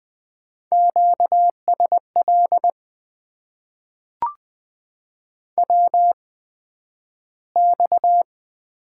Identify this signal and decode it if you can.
Morse code, heard as a keyed tone, about 20 wpm, spaced as for 11 wpm, 700 Hz. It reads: QSL WX
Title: Top 90 QSO Elements - 8x Character Spacing - 20wpm